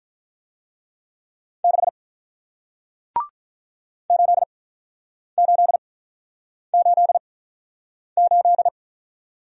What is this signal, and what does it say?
Morse code, heard as a keyed tone, about 47 wpm, spaced as for 9 wpm, 700 Hz. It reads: 6 8888